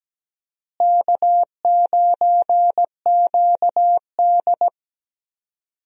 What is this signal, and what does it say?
Morse code, heard as a keyed tone, about 17 wpm, 700 Hz. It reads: K9QD
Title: USA Call Signs - 4 Characters - Repeated 2x - 17wpm